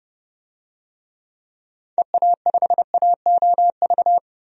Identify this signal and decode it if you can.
Morse code, heard as a keyed tone, about 30 wpm, 700 Hz. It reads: EA5AOV
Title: DXCC-100 Call Signs - 6 Characters - 30wpm